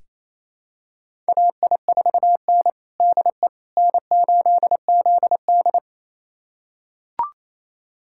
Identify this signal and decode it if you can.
Morse code, heard as a keyed tone, about 28 wpm, 700 Hz. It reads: AI4N DE N8ZD
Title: USA Callsign DE Callsign - 4 Characters - 28wpm